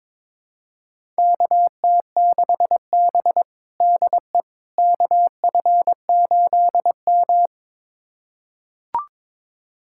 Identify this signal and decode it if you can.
Morse code, heard as a keyed tone, about 22 wpm, 700 Hz. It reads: KT6B DE KF8M
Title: USA Callsign DE Callsign - 4 Characters - 22wpm